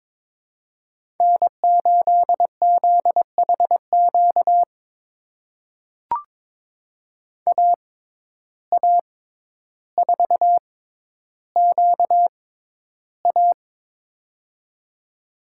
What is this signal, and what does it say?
Morse code, heard as a keyed tone, about 22 wpm, spaced as for 11 wpm, 700 Hz. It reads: N8ZHQ AA4QA
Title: USA Call Signs - 5 Characters - 6x Character Spacing - 22wpm